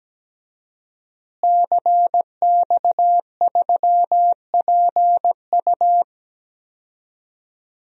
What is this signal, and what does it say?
Morse code, heard as a keyed tone, about 17 wpm, 700 Hz. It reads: CX3PU
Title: DXCC-100 Call Signs - 5 Characters - Repeated 2x - 17wpm